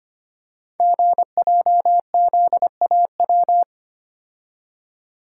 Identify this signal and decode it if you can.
Morse code, heard as a keyed tone, about 25 wpm, 700 Hz. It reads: GJZAW
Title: Code Groups - 5 Letters - 25wpm